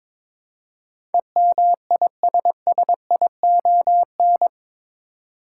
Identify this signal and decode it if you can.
Morse code, heard as a keyed tone, about 22 wpm, 700 Hz. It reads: EMISSION
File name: practice / 22wpm / Eight Letter Words - 22wpm